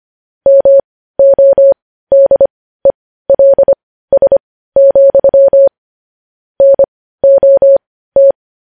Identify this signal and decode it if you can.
Morse code, heard as a keyed tone, about 25 wpm, spaced as for 15 wpm, 550 Hz. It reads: MODELS, NOT
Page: news-headlines-in-morse-code-at-15-wpm